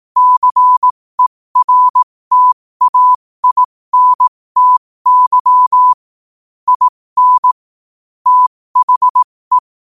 Morse code, heard as a keyed tone, about 18 wpm, spaced as for 15 wpm, 1000 Hz. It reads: CERTAINTY IN THE
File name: qotdfiles / 15wpm